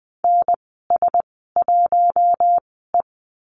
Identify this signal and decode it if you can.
Morse code, heard as a keyed tone, about 20 wpm, spaced as for 14 wpm, 700 Hz. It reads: NS1E
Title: 042 Lesson 8g - Number 1 - Call Signs 20wpm